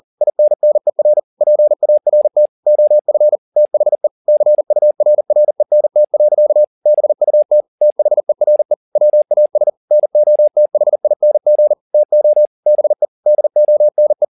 Morse code, heard as a keyed tone, about 40 wpm, 600 Hz. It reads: INNER PART OF THE CURRENT. BUT THERE WAS NOTHING TO BE DONE
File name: the old man and the sea 2 40wpm